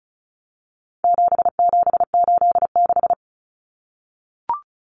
Morse code, heard as a keyed tone, about 34 wpm, 700 Hz. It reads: 7786
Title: Numbers - 4-Digit 35wpm